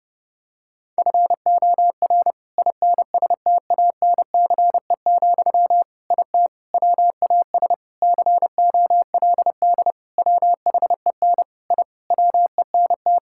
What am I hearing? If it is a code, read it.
Morse code, heard as a keyed tone, about 30 wpm, 700 Hz. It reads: FOR INSTANCE, IT WAS COLD WHEN I WENT
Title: Sentences from Top 1000 Words - Review Part 5 - 30wpm